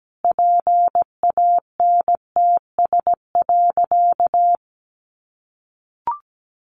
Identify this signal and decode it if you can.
Morse code, heard as a keyed tone, about 17 wpm, 700 Hz. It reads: PANTS.
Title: Additional Easy Phrases 17wpm